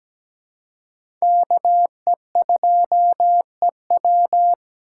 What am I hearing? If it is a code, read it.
Morse code, heard as a keyed tone, about 17 wpm, 700 Hz. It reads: KE2EW
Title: USA Call Signs - 5 Characters - 4x Character Spacing - 17wpm